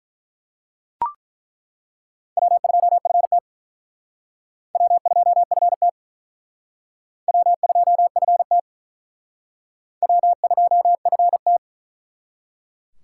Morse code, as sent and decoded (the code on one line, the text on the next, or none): .-- ..--- ..-. - / .-- ..--- ..-. - / .-- ..--- ..-. - / .-- ..--- ..-. -
W2FT W2FT W2FT W2FT